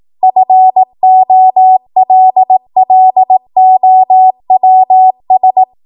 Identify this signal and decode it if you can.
Morse code, heard as a keyed tone, about 18 wpm, 750 Hz. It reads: FOLLOWS
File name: Morse / 18WPM